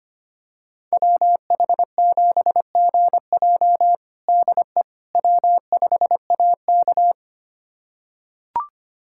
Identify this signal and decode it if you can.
Morse code, heard as a keyed tone, about 25 wpm, 700 Hz. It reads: WH7GJ DE W5AK
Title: USA Callsign DE Callsign 25wpm